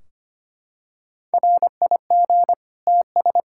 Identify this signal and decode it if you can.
Morse code, heard as a keyed tone, about 25 wpm, 700 Hz. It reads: RIG TS
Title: Practice QSO - Exchange 2 - Info - 25wpm